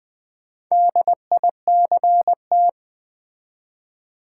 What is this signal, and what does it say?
Morse code, heard as a keyed tone, about 20 wpm, 700 Hz. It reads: DICT
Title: Build A Word - 2 Letters At A Time - 8-Letter Words - 20wpm